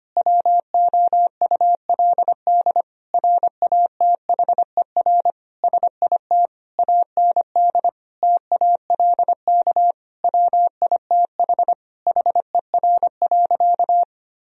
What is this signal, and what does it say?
Morse code, heard as a keyed tone, about 25 wpm, 700 Hz. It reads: WOULD RATHER SIT AND TALK WITH HER.
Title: Sentences from Top 1000 Words - Review Part 4 - 25wpm